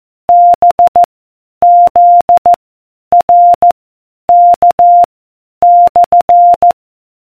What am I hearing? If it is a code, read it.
Morse code, heard as a keyed tone, about 14 wpm, 700 Hz. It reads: B Z R K /